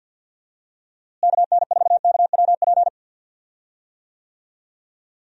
Morse code, heard as a keyed tone, about 48 wpm, 700 Hz. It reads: KN4KWP
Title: USA Call Signs - 6 Characters - Repeated 2x - 50wpm